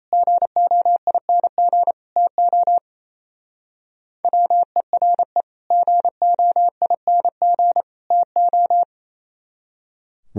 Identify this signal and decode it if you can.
Morse code, heard as a keyed tone, about 32 wpm, 700 Hz. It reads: GOING TO WERE GOING TO
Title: Speed-Racing - Sets of 3 Words 28wpm